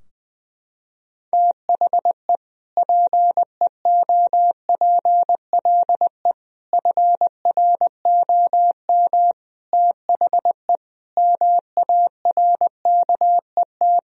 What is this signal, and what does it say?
Morse code, heard as a keyed tone, about 20 wpm, 700 Hz. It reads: THE PEOPLE FROM THE MARKET
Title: Sentences from Top 1000 Words - Review Part 3 - 20wpm